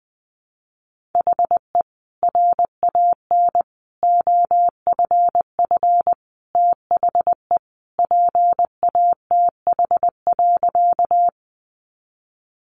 Morse code, heard as a keyed tone, about 20 wpm, 700 Hz. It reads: HE RAN OFF THE PATH.
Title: A Sound Of Thunder 20wpm